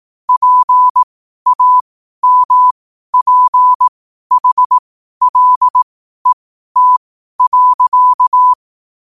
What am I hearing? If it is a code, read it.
Morse code, heard as a keyed tone, about 18 wpm, spaced as for 12 wpm, 1000 Hz. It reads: PAMPHLET.
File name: qotdfiles / 12wpm